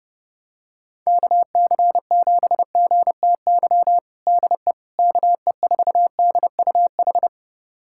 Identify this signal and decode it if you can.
Morse code, heard as a keyed tone, about 30 wpm, 700 Hz. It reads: KC7GTY DE KE4DUH
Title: USA Callsign DE Callsign - 6 Characters - 30wpm